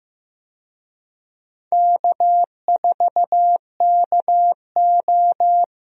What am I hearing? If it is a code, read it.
Morse code, heard as a keyed tone, about 15 wpm, 700 Hz. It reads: K4KO
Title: USA Call Signs - 4 Characters - Repeated 2x - 15wpm